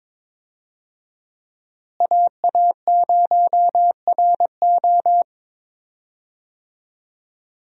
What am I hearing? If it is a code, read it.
Morse code, heard as a keyed tone, about 22 wpm, 700 Hz. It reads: AA0RO
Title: USA Call Signs - 5 Characters - Repeated 3x - 22wpm